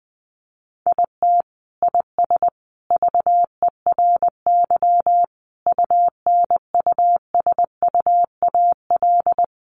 Morse code, heard as a keyed tone, about 20 wpm, 700 Hz. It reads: IT IS VERY UNUSUAL
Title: Sentences from Top 1400 Words 20wpm